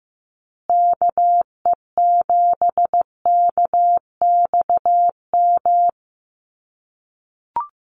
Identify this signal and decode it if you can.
Morse code, heard as a keyed tone, about 15 wpm, 700 Hz. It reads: KE7KXM
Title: USA Call Signs Repeated 2x 15wpm